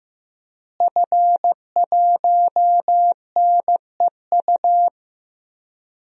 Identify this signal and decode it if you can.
Morse code, heard as a keyed tone, about 15 wpm, 700 Hz. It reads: F1NEU
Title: DXCC-100 Call Signs - 5 Characters - Repeated 2x - 15wpm